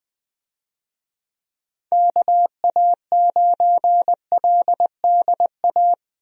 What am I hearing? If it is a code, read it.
Morse code, heard as a keyed tone, about 20 wpm, 700 Hz. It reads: KA9LDA
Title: USA Call Signs - 6 Characters - 20wpm